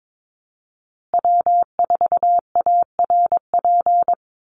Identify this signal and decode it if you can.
Morse code, heard as a keyed tone, about 22 wpm, 700 Hz. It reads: W4ARP
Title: USA Call Signs Repeated 3x 22wpm